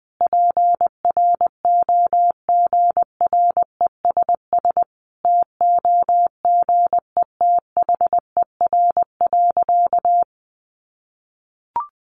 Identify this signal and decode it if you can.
Morse code, heard as a keyed tone, about 20 wpm, 700 Hz. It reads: PROGRESS TOGETHER.
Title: Sentences from Top 1400 Words 20wpm